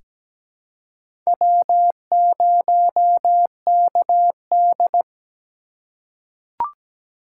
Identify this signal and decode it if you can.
Morse code, heard as a keyed tone, about 17 wpm, 700 Hz. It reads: W0KD
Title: USA Call Signs - 4 Characters - 4x Character Spacing - 17wpm